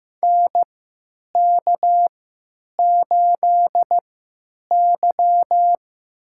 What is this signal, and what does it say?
Morse code, heard as a keyed tone, about 15 wpm, spaced as for 9 wpm, 700 Hz. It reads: NK8Y
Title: USA Call Signs - 4 Characters - 3x Character Spacing - 15wpm